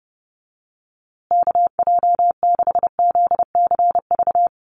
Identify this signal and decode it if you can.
Morse code, heard as a keyed tone, about 30 wpm, 700 Hz. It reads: KJ6ZCV
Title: USA Call Signs Repeated 3x 30wpm